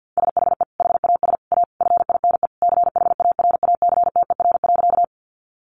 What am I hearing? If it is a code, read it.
Morse code, heard as a keyed tone, about 50 wpm, 700 Hz. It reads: SHE HAS A FINE CHARACTER.
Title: Sentences from Top 1200 Words 50wpm